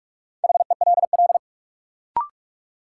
Morse code, heard as a keyed tone, about 46 wpm, 700 Hz. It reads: HELL
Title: Sets of 5 Words Encore - New Word Drill - 45wpm